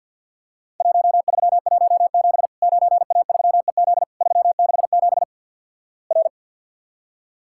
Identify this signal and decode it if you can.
Morse code, heard as a keyed tone, about 50 wpm, 700 Hz. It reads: 1317 9A3EZ 367 R
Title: Worked All Europe DX Contest - QTC Exchange with Repeats - 50wpm